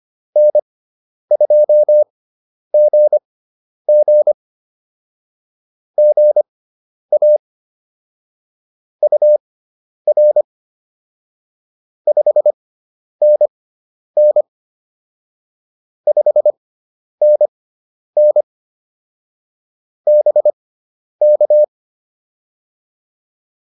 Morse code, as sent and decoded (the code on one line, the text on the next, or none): -. ..--- --. --. / --. .- / ..- .-. / ..... -. -. / ..... -. -. / -... -.-
N2GG GA UR 5NN 5NN BK